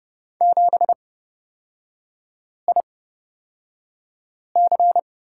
Morse code, heard as a keyed tone, about 31 wpm, spaced as for 5 wpm, 700 Hz. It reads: 7IC
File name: Lesson 38e - Letter q - Three Characters 30wpm